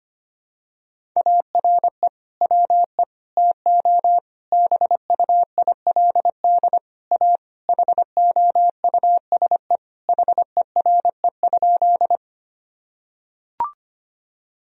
Morse code, heard as a keyed tone, about 25 wpm, 700 Hz. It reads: ARE WE TO BUILD A HOUSE HERE?
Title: Sentences from Top 1000 Words - Review Part 1 - 25wpm